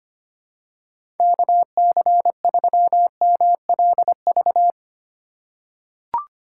Morse code, heard as a keyed tone, about 25 wpm, 700 Hz. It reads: KC3MLV
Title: USA Call Signs - 4x Character Spacing - 25wpm